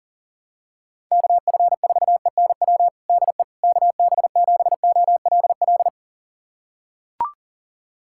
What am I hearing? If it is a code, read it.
Morse code, heard as a keyed tone, about 40 wpm, 700 Hz. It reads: KF4ENW DE KB7OLL
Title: USA Callsign DE Callsign - 6 Characters - 40wpm